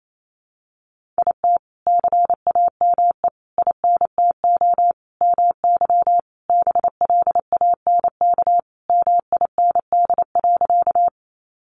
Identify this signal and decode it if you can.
Morse code, heard as a keyed tone, about 28 wpm, 700 Hz. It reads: IT CAME INTO MY BLANK MIND.
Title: Sentences from Top 1500 Words 28wpm